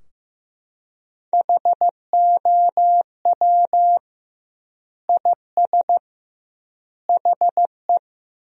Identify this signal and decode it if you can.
Morse code, heard as a keyed tone, about 16 wpm, 700 Hz. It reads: HOW IS HE
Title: 2x Word-Spacing - Sentences from Top 100 Words - 15wpm